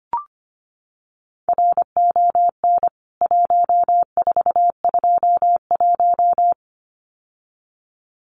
Morse code, heard as a keyed tone, about 25 wpm, 700 Hz. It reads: RON 1421
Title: CWT Exchange 25wpm